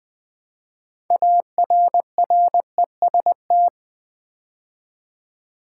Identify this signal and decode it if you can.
Morse code, heard as a keyed tone, about 20 wpm, 700 Hz. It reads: ARREST
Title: Build A Word - 3 Letters At A Time - 6-Letter Words - 20wpm